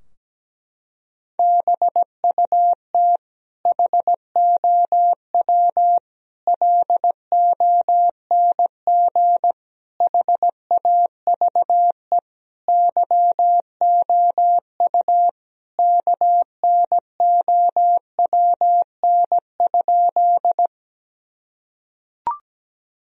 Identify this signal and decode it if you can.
Morse code, heard as a keyed tone, about 17 wpm, 700 Hz. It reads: BUT HOW LONG HAVE YOU KNOWN?
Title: Sentences from Top 1000 Words - Review Part 3 - 17wpm